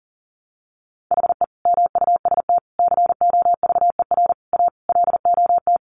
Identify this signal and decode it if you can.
Morse code, heard as a keyed tone, about 37 wpm, 700 Hz. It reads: HE MUST COVER A LOT
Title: Sentences from Top 500 Words 40wpm